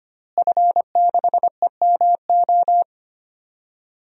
Morse code, heard as a keyed tone, about 25 wpm, 700 Hz. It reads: F6EMO
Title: DXCC-100 Call Signs - 5 Characters - Repeated 3x - 25wpm